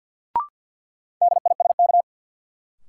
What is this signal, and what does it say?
Morse code, heard as a keyed tone, about 53 wpm, 700 Hz. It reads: DISK
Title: Sentences from Top 1000 Words - Review Part 6 - New Word Drill - 50wpm